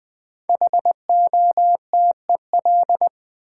Morse code, heard as a keyed tone, about 20 wpm, 700 Hz. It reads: HOTEL
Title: Sentences from Top 1600 Words - New Word Drill - 20wpm